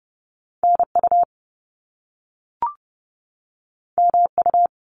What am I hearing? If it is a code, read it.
Morse code, heard as a keyed tone, about 30 wpm, 700 Hz. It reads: NU MU
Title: Top Two Letter Combinations in Words 30wpm